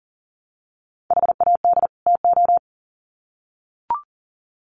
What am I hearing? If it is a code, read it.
Morse code, heard as a keyed tone, about 40 wpm, 700 Hz. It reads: HAD TO
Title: Sets of 2 Words 40wpm